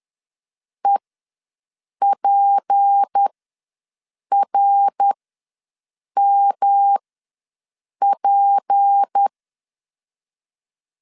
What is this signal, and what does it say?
Morse code, heard as a keyed tone, about 10 wpm, spaced as for 6 wpm, 800 Hz. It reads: EPRMP
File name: LEZIONI / 6 PKX0 / 6 P ws 10wpm 3751